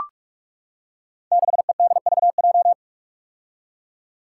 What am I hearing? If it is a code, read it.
Morse code, heard as a keyed tone, about 44 wpm, 700 Hz. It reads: 6EDVJ
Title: Code Groups - 5 Letter:Numbers - 45wpm